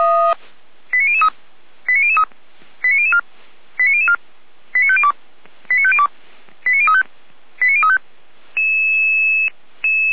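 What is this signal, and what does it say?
Five-tone selective call (ZVEI), heard as quick runs of five tones, tones of about 70 ms. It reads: 89003 89003 89004 89004 89502 89502 89035 89035
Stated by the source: Five-tone ZVEI selective call 89099 received by SDR, on the Feuerwehr Orlamünde fire brigade audio server.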